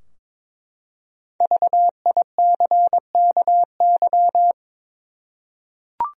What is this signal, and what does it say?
Morse code, heard as a keyed tone, about 22 wpm, 700 Hz. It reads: VICKY
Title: USA Common Names - 22wpm